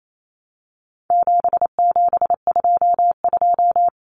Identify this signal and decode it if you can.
Morse code, heard as a keyed tone, about 28 wpm, 700 Hz. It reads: 7722
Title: Numbers - 4-Digit 28wpm